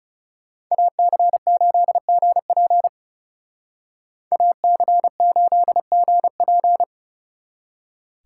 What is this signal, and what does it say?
Morse code, heard as a keyed tone, about 36 wpm, 700 Hz. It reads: AC8GP AC8GP
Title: Speed Racing - USA 2x2 Call Signs 30wpm